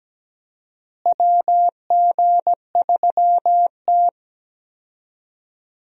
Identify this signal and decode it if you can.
Morse code, heard as a keyed tone, about 17 wpm, 700 Hz. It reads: WG3T
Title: USA Call Signs - 4 Characters - Repeated 2x - 17wpm